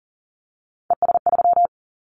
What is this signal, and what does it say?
Morse code, heard as a keyed tone, about 42 wpm, 700 Hz. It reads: ES3
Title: Callsign Prefixes for Top 100 DXCC - 40wpm